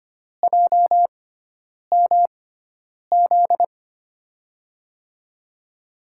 Morse code, heard as a keyed tone, about 25 wpm, spaced as for 9 wpm, 700 Hz. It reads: JMZ
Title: USA Call Signs - 6 Characters - 6x Character Spacing - 25wpm